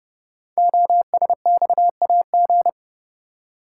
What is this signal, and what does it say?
Morse code, heard as a keyed tone, about 30 wpm, 700 Hz. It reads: OSXAG
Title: Code Groups - 5 Letters - 30wpm